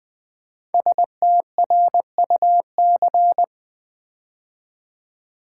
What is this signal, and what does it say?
Morse code, heard as a keyed tone, about 20 wpm, 700 Hz. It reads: STRUC
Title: Build A Word - 5 Letters At A Time - 10-Letter Words - 20wpm